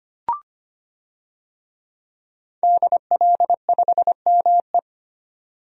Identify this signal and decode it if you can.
Morse code, heard as a keyed tone, about 25 wpm, 700 Hz. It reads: DL5ME
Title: DXCC-100 Call Signs - 5 Characters - 25wpm